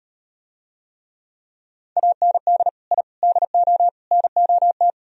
Morse code, heard as a keyed tone, about 38 wpm, 700 Hz. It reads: AND I DO NOT